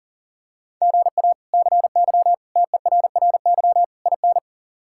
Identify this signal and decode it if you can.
Morse code, heard as a keyed tone, about 40 wpm, 700 Hz. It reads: GA CY TERRY IN
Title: K1USN SST QSO Practice - 40wpm